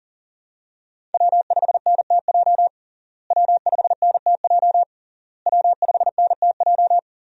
Morse code, heard as a keyed tone, about 38 wpm, 700 Hz. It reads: W5NTJ W5NTJ W5NTJ
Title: USA Call Signs - 5 Characters - Repeated 3x - 40wpm